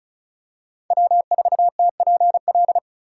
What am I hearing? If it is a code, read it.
Morse code, heard as a keyed tone, about 34 wpm, 700 Hz. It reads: W4TPL
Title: USA Call Signs - 5 Characters - 3x Character Spacing - 35wpm